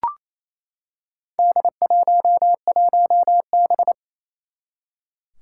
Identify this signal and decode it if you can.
Morse code, heard as a keyed tone, about 28 wpm, 700 Hz. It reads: D11B
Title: Four Challenging Letter-Number Combinations 28wpm